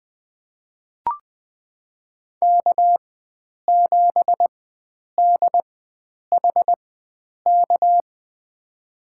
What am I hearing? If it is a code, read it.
Morse code, heard as a keyed tone, about 21 wpm, spaced as for 9 wpm, 700 Hz. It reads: K7DHK
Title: USA Call Signs - 5 Characters - 4x Character Spacing - 20wpm